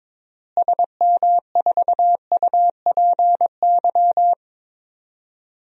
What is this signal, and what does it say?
Morse code, heard as a keyed tone, about 22 wpm, 700 Hz. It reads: SM4UPY
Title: DXCC-100 Call Signs - 6 Characters - Repeated 2x - 22wpm